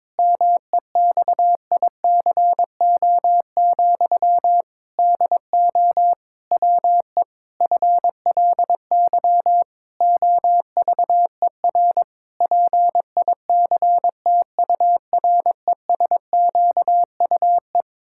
Morse code, heard as a keyed tone, about 22 wpm, 700 Hz. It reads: MEXICO, DO WE FLY OVER PICTURESQUE
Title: Pangram Challenge v2 - 22wpm